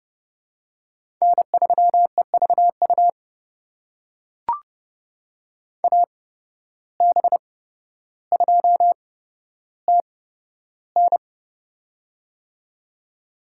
Morse code, heard as a keyed tone, about 32 wpm, spaced as for 13 wpm, 700 Hz. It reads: N3EVU AB2TN
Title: USA Call Signs - 5 Characters - 8x Character Spacing - 30wpm